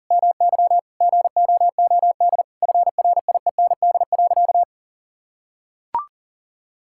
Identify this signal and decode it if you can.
Morse code, heard as a keyed tone, about 40 wpm, 700 Hz. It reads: MY GOOD FRIEND.
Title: Sentences from Top 1600 Words 40wpm